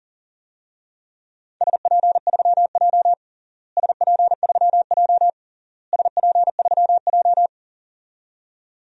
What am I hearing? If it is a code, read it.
Morse code, heard as a keyed tone, about 40 wpm, 700 Hz. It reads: SP3J SP3J SP3J
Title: DXCC-100 Call Signs - 4 Characters - Repeated 3x - 40wpm